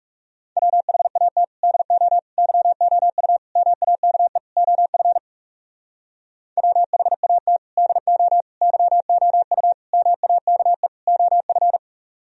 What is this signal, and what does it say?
Morse code, heard as a keyed tone, about 36 wpm, 700 Hz. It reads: WHAT DO YOU MAKE OF WHAT DO YOU MAKE OF
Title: Speed-Racing - Sets of 5 Words 30wpm